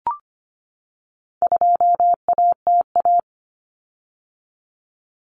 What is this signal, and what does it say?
Morse code, heard as a keyed tone, about 25 wpm, 700 Hz. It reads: 2ATA
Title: Common Cut Numbers - 4-Digit 25wpm